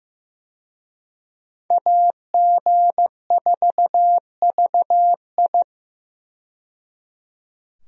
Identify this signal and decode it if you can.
Morse code, heard as a keyed tone, about 15 wpm, 700 Hz. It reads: AG4VI
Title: USA Call Signs - 5 Characters - 15wpm